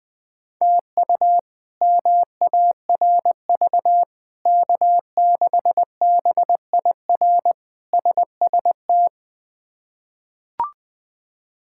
Morse code, heard as a keyed tone, about 20 wpm, 700 Hz. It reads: TU MARV K6BIR SST
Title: K1USN SST QSO Practice - 20wpm